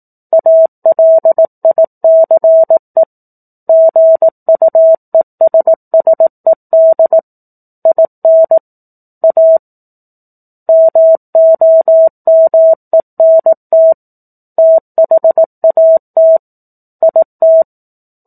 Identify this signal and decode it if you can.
Morse code, heard as a keyed tone, about 18 wpm, 650 Hz. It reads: ALICE GUESSED IN A MOMENT THAT IT
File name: 04-Chapter-04 18 WPM